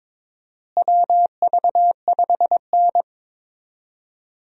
Morse code, heard as a keyed tone, about 22 wpm, 700 Hz. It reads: WV5N
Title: USA Call Signs - 4 Characters - Repeated 3x - 22wpm